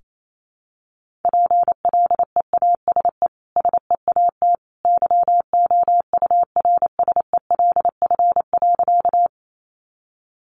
Morse code, heard as a keyed tone, about 28 wpm, 700 Hz. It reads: PLEASE SEAT YOURSELF.
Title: Sentences from Top 1500 Words 28wpm